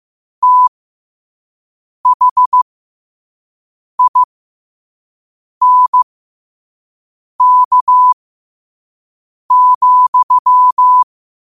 Morse code, heard as a keyed tone, about 15 wpm, spaced as for 5 wpm, 1000 Hz. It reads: THINK,
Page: New quotes every day in morse code at 5 Words per minute.